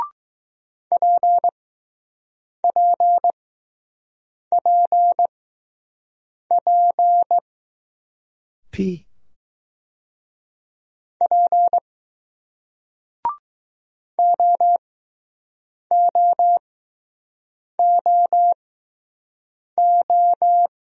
Morse code, heard as a keyed tone, about 20 wpm, spaced as for 6 wpm, 700 Hz. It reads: PPPP P OOOO